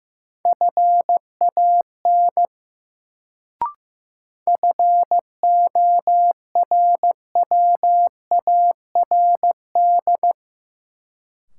Morse code, heard as a keyed tone, about 15 wpm, 700 Hz. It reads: FAN FORWARD
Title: Sentences from Top 1300 Words - New Word Drill Alt - 15wpm